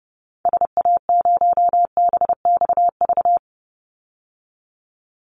Morse code, heard as a keyed tone, about 30 wpm, 700 Hz. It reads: SA0BXV
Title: Call Signs Repeated 3x - 30wpm